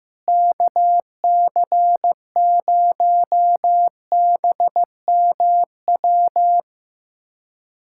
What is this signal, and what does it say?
Morse code, heard as a keyed tone, about 15 wpm, 700 Hz. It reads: KC0BMW
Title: USA Callsign DE Callsign - 6 Characters - 15wpm